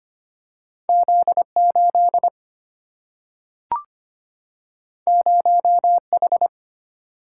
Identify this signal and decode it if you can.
Morse code, heard as a keyed tone, about 25 wpm, 700 Hz. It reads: Z 8 0 H
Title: Single-Letter Single-Number 25wpm